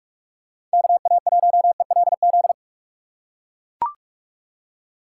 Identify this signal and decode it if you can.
Morse code, heard as a keyed tone, about 41 wpm, 700 Hz. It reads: KA1ELZ
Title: USA Call Signs - 6 Characters - 6x Character Spacing - 45wpm